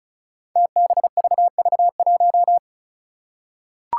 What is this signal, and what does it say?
Morse code, heard as a keyed tone, about 34 wpm, 700 Hz. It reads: TBVV1
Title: Five Challenging Letter-Number Combinations 35wpm